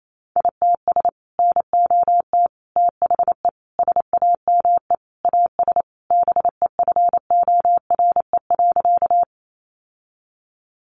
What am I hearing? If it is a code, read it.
Morse code, heard as a keyed tone, about 28 wpm, 700 Hz. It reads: ITS NOT THE SAME AS BEFORE.
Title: Sentences from Top 600 Words 28wpm